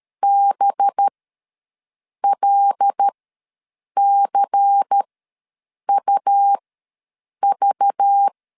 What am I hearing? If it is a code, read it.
Morse code, heard as a keyed tone, about 13 wpm, spaced as for 7 wpm, 800 Hz. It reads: BLCUV
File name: LEZIONI / 6 PKX0 / 6 P ws 12wpm 3117